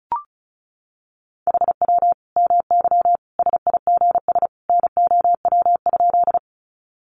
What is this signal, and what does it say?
Morse code, heard as a keyed tone, about 34 wpm, 700 Hz. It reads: HW MY SIGS NOW?
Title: Common QSO Phrases 35wpm